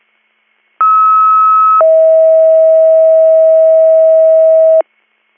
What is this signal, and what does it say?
Two-tone page, heard as a one-second tone, then a three-second tone, about 1290 then 640 Hz. It is heard from 0.8 to 4.8 s.